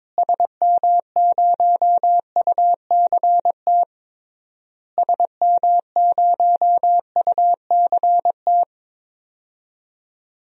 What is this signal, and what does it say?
Morse code, heard as a keyed tone, about 22 wpm, 700 Hz. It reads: SM0UCT SM0UCT
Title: DXCC-100 Call Signs - 6 Characters - Repeated 3x - 22wpm